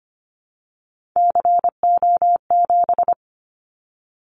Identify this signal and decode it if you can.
Morse code, heard as a keyed tone, about 25 wpm, 700 Hz. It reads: CO7
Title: Callsign Prefixes for Top 100 DXCC - 25wpm